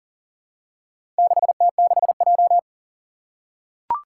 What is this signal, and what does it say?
Morse code, heard as a keyed tone, about 38 wpm, 700 Hz. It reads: 6T6J
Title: Four Challenging Letter-Number Combinations 40wpm